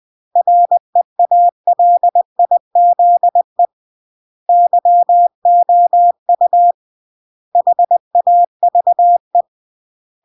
Morse code, heard as a keyed tone, about 20 wpm, 700 Hz. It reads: REALIZE YOU HAVE
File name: CW / jodorowsky / 82 commandments 20-20